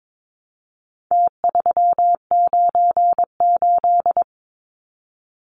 Morse code, heard as a keyed tone, about 22 wpm, 700 Hz. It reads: T398
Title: Numbers - Contest Style - 4-Digit 22wpm